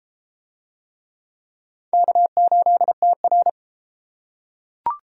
Morse code, heard as a keyed tone, about 33 wpm, 700 Hz. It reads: K8TR